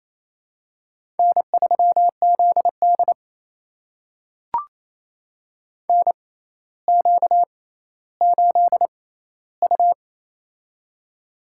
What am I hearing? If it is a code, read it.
Morse code, heard as a keyed tone, about 28 wpm, spaced as for 14 wpm, 700 Hz. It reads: N3ZD NQ8U
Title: USA Call Signs - 4 Characters - 6x Character Spacing - 28wpm